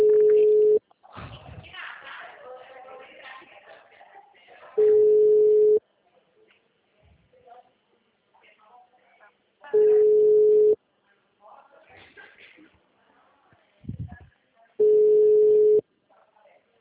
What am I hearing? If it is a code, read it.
Ringback tone (Europe, 425 Hz), with speech.
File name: out- -2009- - - .149